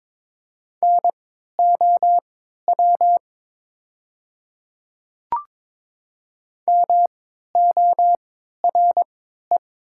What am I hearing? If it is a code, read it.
Morse code, heard as a keyed tone, about 22 wpm, spaced as for 12 wpm, 700 Hz. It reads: NOW MORE
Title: 3x Character Spacing - Top 200 Words - 22wpm